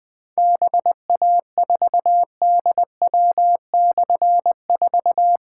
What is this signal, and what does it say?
Morse code, heard as a keyed tone, about 20 wpm, 700 Hz. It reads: BA4DW/4
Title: Difficult Call Signs 20wpm